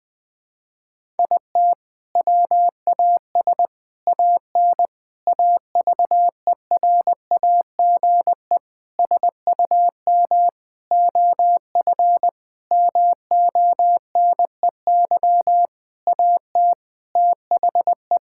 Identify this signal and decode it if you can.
Morse code, heard as a keyed tone, about 20 wpm, 700 Hz. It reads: IT WAS AN AVERAGE SUM OF MONEY AT THE
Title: Sentences from Top 1000 Words - Review Part 6 - 20wpm